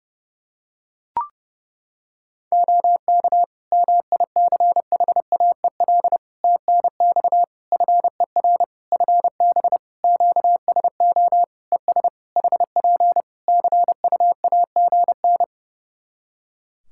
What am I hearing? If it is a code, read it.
Morse code, heard as a keyed tone, about 32 wpm, 700 Hz. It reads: OK MICHAEL TNX FER FB QSO ES HP CUAGN
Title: Practice QSO - Exchange 4 - Ending - 30wpm